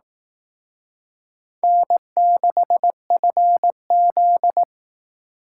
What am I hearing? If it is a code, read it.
Morse code, heard as a keyed tone, about 18 wpm, 700 Hz. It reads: N6FZ
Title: Speed Racing - USA 1x2 Call Signs 15wpm